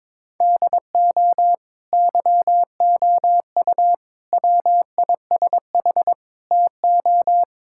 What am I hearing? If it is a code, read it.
Morse code, heard as a keyed tone, about 22 wpm, 700 Hz. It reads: DO YOU WISH TO
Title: Sentences from Top 1600 Words 22wpm